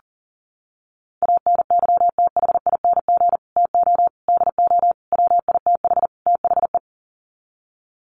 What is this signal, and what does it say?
Morse code, heard as a keyed tone, about 40 wpm, 700 Hz. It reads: ANYTHING TO DO WITH THE
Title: Sets of 5 Words Encore 40wpm